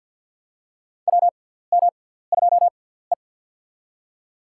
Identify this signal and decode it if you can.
Morse code, heard as a keyed tone, about 48 wpm, spaced as for 17 wpm, 700 Hz. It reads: WM2E
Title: USA Call Signs - 4 Characters - 6x Character Spacing - 50wpm